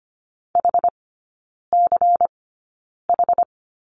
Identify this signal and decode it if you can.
Morse code, heard as a keyed tone, about 25 wpm, spaced as for 9 wpm, 700 Hz. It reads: HCH
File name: course / 25:10wpm / 093 Lesson 16e - Letter c - Three Characters 25wpm